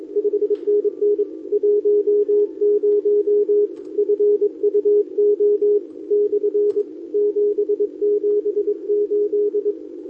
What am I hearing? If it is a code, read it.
Morse code, heard as a keyed tone, about 23 wpm, 410 Hz. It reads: 5NN 10 FUO / 778
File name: OK QRP 2026 / OKQRP2026